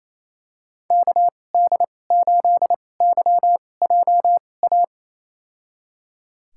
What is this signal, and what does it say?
Morse code, heard as a keyed tone, about 28 wpm, spaced as for 20 wpm, 700 Hz. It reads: KD8YJA
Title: USA Call Signs - 6 Characters - 2x Character Spacing - 28wpm